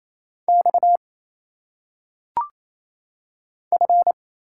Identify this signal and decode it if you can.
Morse code, heard as a keyed tone, about 27 wpm, 700 Hz. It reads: X F
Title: Single Letters - Phonetic Pronunciation 28wpm